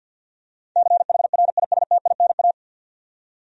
Morse code, heard as a keyed tone, about 48 wpm, 700 Hz. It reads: CHRISTINA
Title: USA Common Names - 50wpm